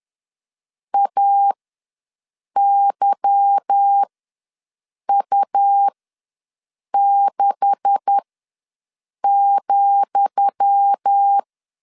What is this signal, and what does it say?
Morse code, heard as a keyed tone, about 10 wpm, spaced as for 6 wpm, 800 Hz. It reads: AYU6,
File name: LEZIONI / 10 SIGNS / 10D COMA ws 10wpm 4314